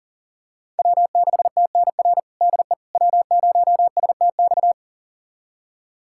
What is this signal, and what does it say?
Morse code, heard as a keyed tone, about 37 wpm, 700 Hz. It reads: W6TNR DE W0STX
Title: USA Callsign DE Callsign - 5 Characters - 40wpm